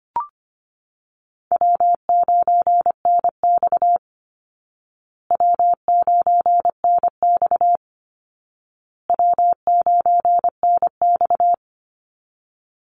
Morse code, heard as a keyed tone, about 25 wpm, 700 Hz. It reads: W9NX W9NX W9NX
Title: Easy Call Signs Repeated 3x - 25wpm